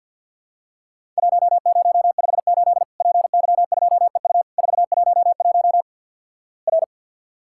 Morse code, heard as a keyed tone, about 53 wpm, 700 Hz. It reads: 1058 PY2EU 411 R
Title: Worked All Europe DX Contest - QTC Exchange with Repeats - 50wpm